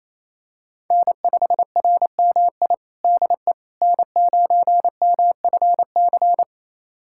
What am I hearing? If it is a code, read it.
Morse code, heard as a keyed tone, about 27 wpm, 700 Hz. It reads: N5RMI DE N9MFC
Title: USA Callsign DE Callsign - 5 Characters - 28wpm